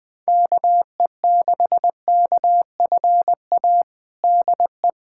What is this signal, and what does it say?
Morse code, heard as a keyed tone, about 20 wpm, 700 Hz. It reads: KE6KFA DE
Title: USA Callsign DE Callsign - 6 Characters - 20wpm